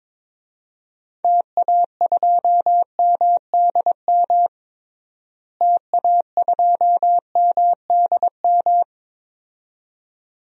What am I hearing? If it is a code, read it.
Morse code, heard as a keyed tone, about 22 wpm, 700 Hz. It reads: TA2MDM TA2MDM
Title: DXCC-100 Call Signs - 6 Characters - Repeated 2x - 22wpm